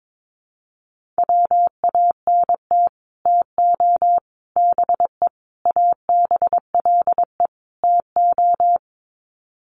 Morse code, heard as a keyed tone, about 22 wpm, 700 Hz. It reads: WANT TO BE ABLE TO
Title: Sets of 5 Words 22wpm